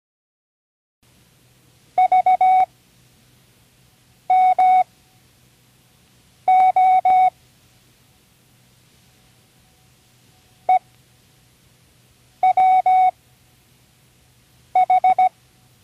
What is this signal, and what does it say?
Morse code, heard as a keyed tone, about 17 wpm, spaced as for 5 wpm, 740 Hz. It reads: VMO EWH
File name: Learning Morse Code (from Tune in the World)